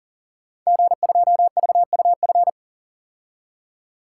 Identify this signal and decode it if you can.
Morse code, heard as a keyed tone, about 41 wpm, 700 Hz. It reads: G2VUF
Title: Code Groups - 5 Letter:Numbers - 40wpm